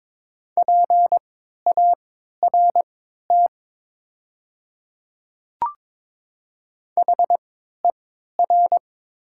Morse code, heard as a keyed tone, about 22 wpm, spaced as for 13 wpm, 700 Hz. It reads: PART HER
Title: 3x Character Spacing - Top 200 Words - 22wpm